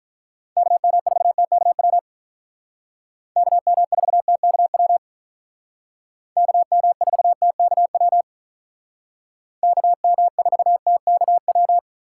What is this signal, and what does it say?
Morse code, heard as a keyed tone, about 43 wpm, 700 Hz. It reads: KM4TKW KM4TKW KM4TKW KM4TKW
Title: Speed Racing - USA 2x3 Call Signs 35wpm